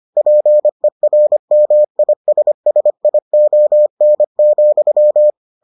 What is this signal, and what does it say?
Morse code, heard as a keyed tone, about 25 wpm, 600 Hz. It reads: PERMISSION,